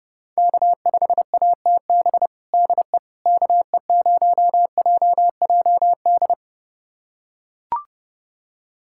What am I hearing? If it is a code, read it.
Morse code, heard as a keyed tone, about 30 wpm, 700 Hz. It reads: K5ATB DE KE0JJD
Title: USA Callsign DE Callsign 30wpm